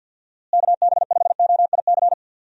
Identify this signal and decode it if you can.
Morse code, heard as a keyed tone, about 52 wpm, 700 Hz. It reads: KB5OIC
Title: USA Call Signs - 6 Characters - Repeated 2x - 50wpm